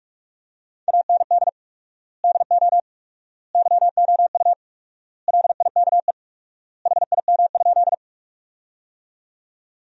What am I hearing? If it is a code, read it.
Morse code, heard as a keyed tone, about 48 wpm, 700 Hz. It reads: AND DO YOU LIKE HIM?
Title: 4x Word-Spacing - Sentences from Top 100 Words - 45wpm